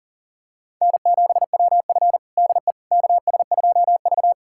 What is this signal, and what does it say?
Morse code, heard as a keyed tone, about 40 wpm, 700 Hz. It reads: N7WF DE KS2V
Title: USA Callsign DE Callsign - 4 Characters - 40wpm